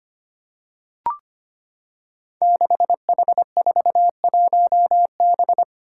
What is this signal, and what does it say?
Morse code, heard as a keyed tone, about 25 wpm, 700 Hz. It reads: 6H41B
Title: Five Challenging Letter-Number Combinations 25wpm